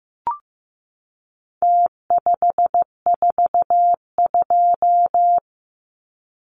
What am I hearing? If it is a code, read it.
Morse code, heard as a keyed tone, about 15 wpm, 700 Hz. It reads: T542
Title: Numbers - Contest Style - 4-Digit 15wpm